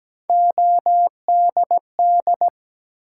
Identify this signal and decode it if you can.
Morse code, heard as a keyed tone, about 17 wpm, 700 Hz. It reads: ODD
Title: Three Letter Words - 17wpm